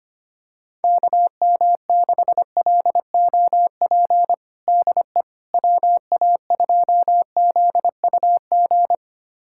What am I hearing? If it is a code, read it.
Morse code, heard as a keyed tone, about 25 wpm, 700 Hz. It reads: KM6LOP DE WA2ZUG
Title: USA Callsign DE Callsign - 6 Characters - 25wpm